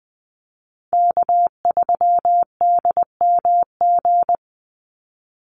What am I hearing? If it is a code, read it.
Morse code, heard as a keyed tone, about 20 wpm, 700 Hz. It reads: K3DMG
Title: Call Signs Repeated 2x - 20wpm